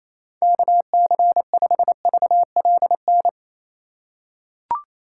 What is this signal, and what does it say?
Morse code, heard as a keyed tone, about 28 wpm, 700 Hz. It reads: KC5VLN
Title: USA Call Signs - 2x Character Spacing - 28wpm